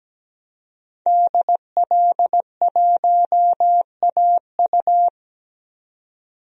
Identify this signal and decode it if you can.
Morse code, heard as a keyed tone, about 17 wpm, 700 Hz. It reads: DL1AU
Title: DXCC-100 Call Signs - 5 Characters - 17wpm